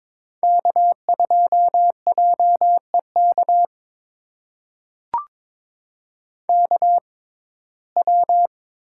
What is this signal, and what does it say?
Morse code, heard as a keyed tone, about 22 wpm, spaced as for 16 wpm, 700 Hz. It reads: K2JEK KW
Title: USA Call Signs - 6x Character Spacing - 22wpm